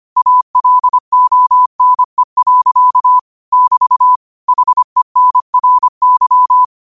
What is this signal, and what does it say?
Morse code, heard as a keyed tone, about 25 wpm, 1000 Hz. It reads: ALONE. <BT> HENRY
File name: qotdfiles / 25wpm